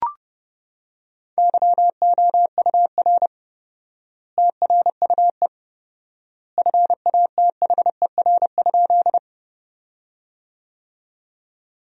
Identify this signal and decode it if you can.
Morse code, heard as a keyed tone, about 30 wpm, 700 Hz. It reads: YOUR TRUE FATHER?
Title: 4x Word-Spacing - Sentences from Top 400 Words - 30wpm